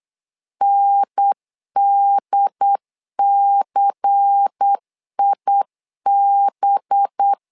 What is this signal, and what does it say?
Morse code, heard as a keyed tone, about 8 wpm, 800 Hz. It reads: NDCIB
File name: LEZIONI / 5 LBVC / 5 LBVC 8wpm 2935